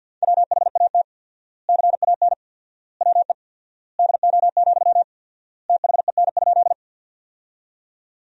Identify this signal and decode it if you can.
Morse code, heard as a keyed tone, about 48 wpm, 700 Hz. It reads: WHAT CAN WE DO, THEN?
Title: 4x Word-Spacing - Sentences from Top 100 Words - 50wpm